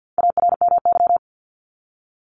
Morse code, heard as a keyed tone, about 50 wpm, 700 Hz. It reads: ARMY
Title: Top 501-750 Words 50wpm